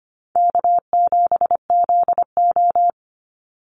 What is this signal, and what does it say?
Morse code, heard as a keyed tone, about 25 wpm, 700 Hz. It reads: K7ZO
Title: NA SOTA Chaser Callsigns 25wpm